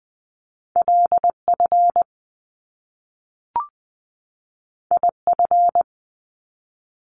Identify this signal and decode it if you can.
Morse code, heard as a keyed tone, about 20 wpm, 700 Hz. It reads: LF IF
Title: Top Two Letter Combinations in Words 20wpm